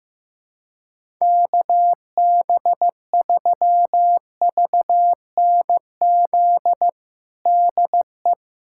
Morse code, heard as a keyed tone, about 15 wpm, 700 Hz. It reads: KB3VNZ DE
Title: USA Callsign DE Callsign 15wpm